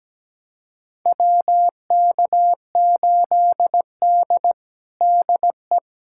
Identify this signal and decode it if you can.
Morse code, heard as a keyed tone, about 17 wpm, 700 Hz. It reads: WK8D DE
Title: USA Callsign DE Callsign - 4 Characters - 17wpm